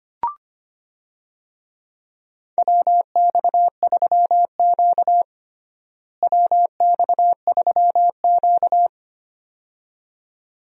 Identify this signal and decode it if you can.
Morse code, heard as a keyed tone, about 25 wpm, 700 Hz. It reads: WX3Q WX3Q
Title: USA Call Signs - 4 Characters - Repeated 2x - 25wpm